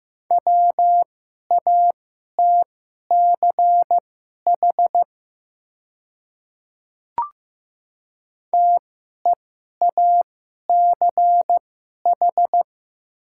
Morse code, heard as a keyed tone, about 15 wpm, spaced as for 11 wpm, 700 Hz. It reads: WATCH TEACH
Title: 2x Character Spacing - Top 500 Words - 15wpm